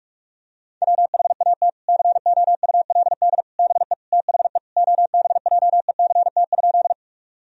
Morse code, heard as a keyed tone, about 41 wpm, 700 Hz. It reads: WHAT COULD BE THE OBJECT?
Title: Sentences from Top 1000 Words - Review Part 3 - 45wpm